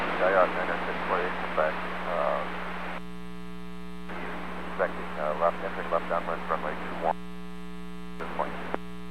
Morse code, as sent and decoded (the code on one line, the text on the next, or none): none